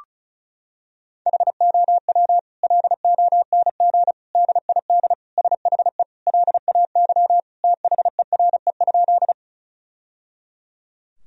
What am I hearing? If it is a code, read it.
Morse code, heard as a keyed tone, about 36 wpm, 700 Hz. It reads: HOW LONG DID SHE LAY THERE?
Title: Sentences from Top 1000 Words - Review Part 2 - 35wpm